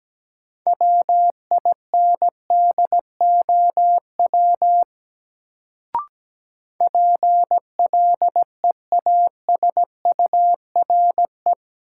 Morse code, heard as a keyed tone, about 17 wpm, 700 Hz. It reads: WINDOW PLEASURE
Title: Sentences from Top 1600 Words - New Word Drill - 17wpm